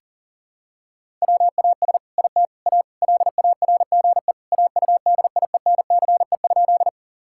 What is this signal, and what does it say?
Morse code, heard as a keyed tone, about 40 wpm, 700 Hz. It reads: WAS IT A LARGE AUDIENCE?
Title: Sentences from Top 1000 Words - Review Part 5 - 40wpm